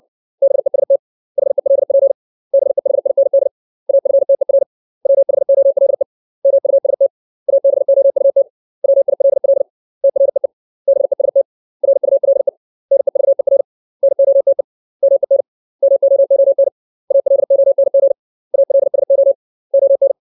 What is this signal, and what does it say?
Morse code, heard as a keyed tone, about 60 wpm, 550 Hz. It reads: BEST HELP BEHIND AFTER WHOLE MUST ABOUT WILL TREE BEST RULE NEVER NOTE MEN MOON ALONG ALSO ON
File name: 632 words / 632 Words 60WPM